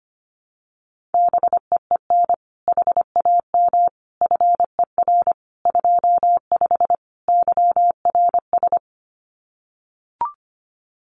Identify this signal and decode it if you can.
Morse code, heard as a keyed tone, about 25 wpm, 700 Hz. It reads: BEEN HAM FER 25 YRS
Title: Common QSO Phrases Encore 25wpm